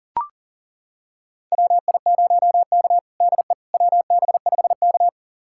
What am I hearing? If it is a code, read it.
Morse code, heard as a keyed tone, about 40 wpm, 700 Hz. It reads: WI0K DE WB5K
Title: USA Callsign DE Callsign - 4 Characters - 40wpm